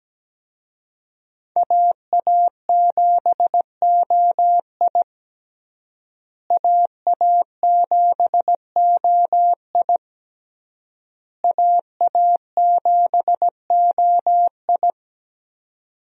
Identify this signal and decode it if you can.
Morse code, heard as a keyed tone, about 17 wpm, 700 Hz. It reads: AA7OI AA7OI AA7OI
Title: USA Call Signs - 5 Characters - Repeated 3x - 17wpm